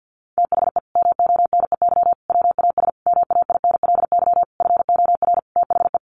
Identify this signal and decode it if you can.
Morse code, heard as a keyed tone, about 50 wpm, 700 Hz. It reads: THE MONEY WAS MAINLY FOR THE
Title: Sentences from Top 1400 Words 50wpm